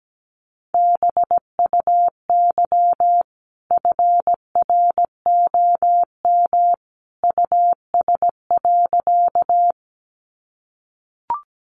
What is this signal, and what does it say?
Morse code, heard as a keyed tone, about 17 wpm, 700 Hz. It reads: BUY FROM US.
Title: Sentences from Top 700 Words 17wpm